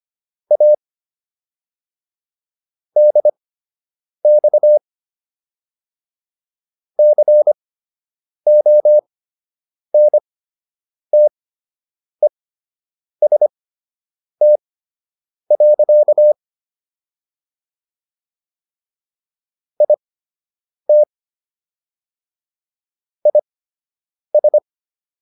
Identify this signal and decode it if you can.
Morse code, heard as a keyed tone, about 25 wpm, spaced as for 8 wpm, 600 Hz. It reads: A DX CONTEST. IT IS